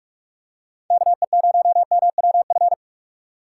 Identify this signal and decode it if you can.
Morse code, heard as a keyed tone, about 42 wpm, 700 Hz. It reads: KE0MWF
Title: USA Call Signs - 3x Character Spacing - 45wpm